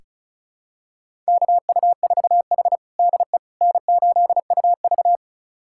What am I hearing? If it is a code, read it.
Morse code, heard as a keyed tone, about 34 wpm, 700 Hz. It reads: KU4H DE N8UV
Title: USA Callsign DE Callsign - 4 Characters - 35wpm